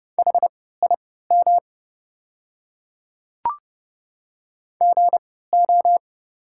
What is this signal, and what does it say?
Morse code, heard as a keyed tone, about 31 wpm, spaced as for 17 wpm, 700 Hz. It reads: HIM GO
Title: 3x Character Spacing - Top 500 Words - 30wpm